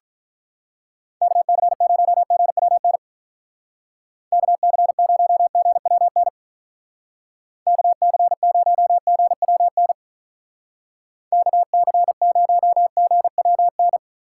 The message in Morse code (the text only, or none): KC0GWN KC0GWN KC0GWN KC0GWN